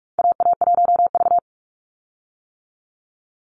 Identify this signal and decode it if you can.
Morse code, heard as a keyed tone, about 44 wpm, 700 Hz. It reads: AA1V
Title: Call Signs - 45wpm